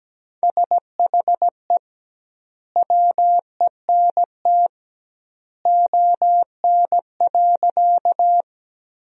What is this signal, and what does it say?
Morse code, heard as a keyed tone, about 17 wpm, 700 Hz. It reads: SHE WENT ON.
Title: 2x Word-Spacing - Sentences from Top 200 Words - 17wpm